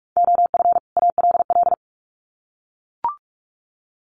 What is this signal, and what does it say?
Morse code, heard as a keyed tone, about 44 wpm, 700 Hz. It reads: OF ALL
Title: Sets of 2 Words 45wpm